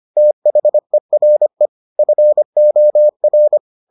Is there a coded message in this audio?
Yes: Morse code from 0.2 to 3.6 s, heard as a keyed tone, about 25 wpm, 600 Hz. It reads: THERE FOR